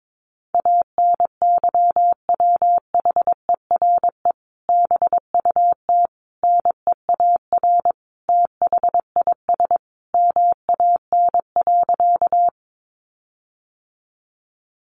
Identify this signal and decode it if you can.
Morse code, heard as a keyed tone, about 22 wpm, 700 Hz. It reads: ANYWHERE BUT NEAR THIS MAN.
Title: Sentences from Top 1400 Words 22wpm